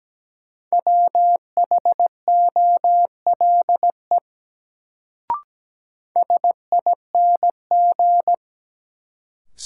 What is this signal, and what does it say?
Morse code, heard as a keyed tone, about 17 wpm, 700 Hz. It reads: WHOLE SING
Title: Sentences from Top 400 Words - New Word Drill - 17wpm